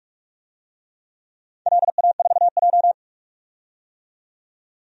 Morse code, heard as a keyed tone, about 41 wpm, 700 Hz. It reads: LA4J
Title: DXCC-100 Call Signs - 4 Characters - 45wpm